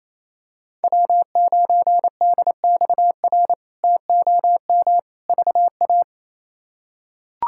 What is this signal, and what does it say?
Morse code, heard as a keyed tone, about 28 wpm, 700 Hz. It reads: W9DXR TOM VA
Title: K1USN SST QSO Practice - 28wpm